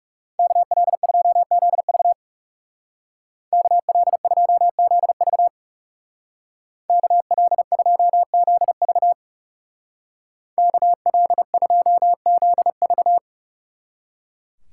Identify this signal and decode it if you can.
Morse code, heard as a keyed tone, about 37 wpm, 700 Hz. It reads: KL2ZV KL2ZV KL2ZV KL2ZV
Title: Speed Racing - USA Call Signs 30wpm